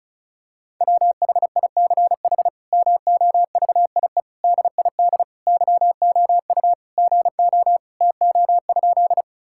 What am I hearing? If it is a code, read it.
Morse code, heard as a keyed tone, about 34 wpm, 700 Hz. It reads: WHICH MOVIE DID YOU GO TO?
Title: Sentences from Top 1000 Words - Review Part 5 - 35wpm